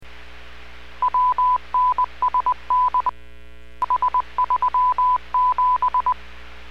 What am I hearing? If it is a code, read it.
Morse code, heard as a keyed tone, about 20 wpm, 1000 Hz. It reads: WNSN S37